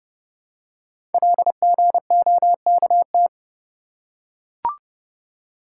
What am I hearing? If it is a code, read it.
Morse code, heard as a keyed tone, about 30 wpm, 700 Hz. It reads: LGOKT
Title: Code Groups - 5 Letters - 30wpm